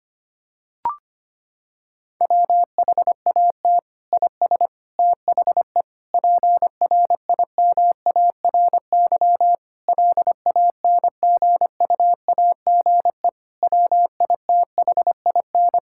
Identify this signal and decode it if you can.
Morse code, heard as a keyed tone, about 25 wpm, 700 Hz. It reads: WHAT IS THE PRIMARY LANGUAGE WITHIN
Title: Sentences from Top 1000 Words - Review Part 5 - 25wpm